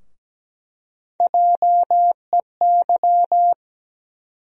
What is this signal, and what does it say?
Morse code, heard as a keyed tone, about 17 wpm, 700 Hz. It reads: JEY
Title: USA Call Sign Postfixes - 3 Letters 17wpm